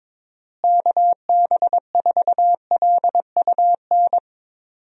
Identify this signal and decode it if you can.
Morse code, heard as a keyed tone, about 22 wpm, 700 Hz. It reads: KB4LUN
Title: USA Call Signs - 6 Characters - Repeated 2x - 22wpm